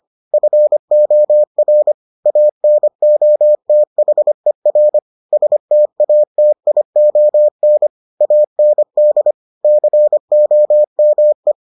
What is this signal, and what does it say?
Morse code, heard as a keyed tone, about 25 wpm, 600 Hz. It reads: FOR ANOTHER STATION AND COME